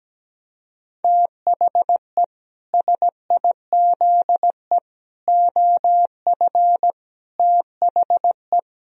Morse code, heard as a keyed tone, about 17 wpm, 700 Hz. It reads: THE SIZE OF THE
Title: Sentences from Top 1000 Words - Review Part 2 - 17wpm